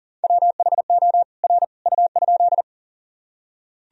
Morse code, heard as a keyed tone, about 39 wpm, 700 Hz. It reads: WHO R U?
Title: Sentences from Top 100 Words and Abbreviations 40wpm